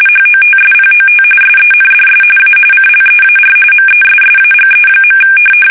RTTY, about 45.45 baud. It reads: HOTSPOT SENDS A CLEAR MESSAGE.
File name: fsk850 / battleshipnj